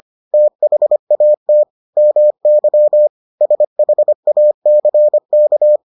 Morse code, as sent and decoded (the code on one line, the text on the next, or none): - .... .- - / -- -.-- / ... .... .- -.-. -.-
THAT MY SHACK